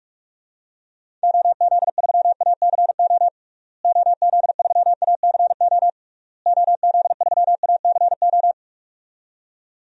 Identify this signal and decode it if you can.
Morse code, heard as a keyed tone, about 46 wpm, 700 Hz. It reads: OZ3ACO OZ3ACO OZ3ACO
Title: DXCC-100 Call Signs - 6 Characters - Repeated 3x - 45wpm